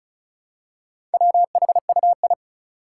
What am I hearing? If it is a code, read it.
Morse code, heard as a keyed tone, about 36 wpm, 700 Hz. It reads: WHUI
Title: Four Challenging Letter-Number Combinations 35wpm